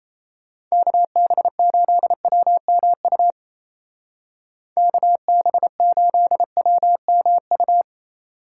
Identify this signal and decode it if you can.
Morse code, heard as a keyed tone, about 34 wpm, 700 Hz. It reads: KB8WMU KB8WMU
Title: Speed Racing - USA 2x3 Call Signs 28wpm